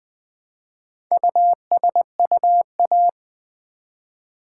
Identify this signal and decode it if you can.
Morse code, heard as a keyed tone, about 20 wpm, 700 Hz. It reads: USUA
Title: Build A Word - 1 Letter At A Time - 5-Letter Words - 20wpm